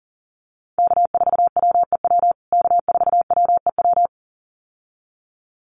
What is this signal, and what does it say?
Morse code, heard as a keyed tone, about 40 wpm, 700 Hz. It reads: K4WEW K4WEW
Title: USA Call Signs Repeated 2x 40wpm